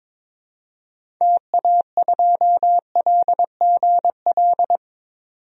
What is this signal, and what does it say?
Morse code, heard as a keyed tone, about 22 wpm, 700 Hz. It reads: TA2LGL
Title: DXCC-100 Call Signs - 6 Characters - 22wpm